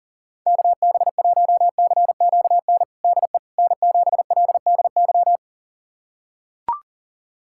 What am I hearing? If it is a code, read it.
Morse code, heard as a keyed tone, about 42 wpm, 700 Hz. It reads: KB1CQN DE N7LDY
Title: USA Callsign DE Callsign 40wpm